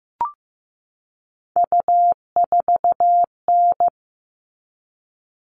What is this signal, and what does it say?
Morse code, heard as a keyed tone, about 15 wpm, 700 Hz. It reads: U4N
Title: All Cut Numbers - 3-Digit 15wpm